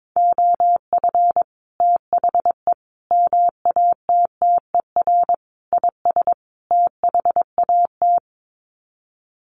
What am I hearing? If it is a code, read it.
Morse code, heard as a keyed tone, about 22 wpm, 700 Hz. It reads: OF THE MATTER IS THAT
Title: Sets of 5 Words Encore 22wpm